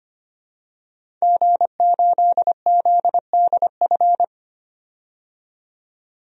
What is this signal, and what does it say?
Morse code, heard as a keyed tone, about 25 wpm, 700 Hz. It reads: G8ZDF
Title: DXCC-100 Call Signs - 5 Characters - 25wpm